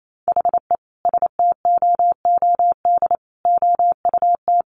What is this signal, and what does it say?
Morse code, heard as a keyed tone, about 28 wpm, 700 Hz. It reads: HE STOOD OUT
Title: Sentences from Top 1500 Words 28wpm